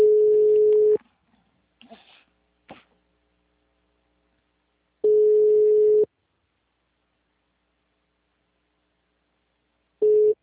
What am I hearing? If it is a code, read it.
Ringback tone (Europe, 425 Hz), with speech.